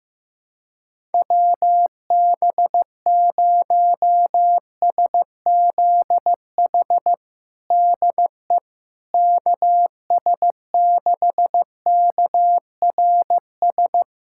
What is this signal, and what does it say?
Morse code, heard as a keyed tone, about 15 wpm, 700 Hz. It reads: WB0SZH DE KS6KRS
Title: USA Callsign DE Callsign 15wpm